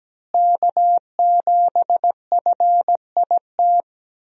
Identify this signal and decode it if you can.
Morse code, heard as a keyed tone, about 17 wpm, 700 Hz. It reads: K7FIT
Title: USA Call Signs - 5 Characters - Repeated 3x - 17wpm